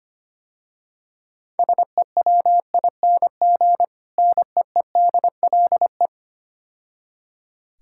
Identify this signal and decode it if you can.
Morse code, heard as a keyed tone, about 26 wpm, 700 Hz. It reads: SEWING NEEDLE
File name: Easy Words - Speed-Racing with Hints - 25wpm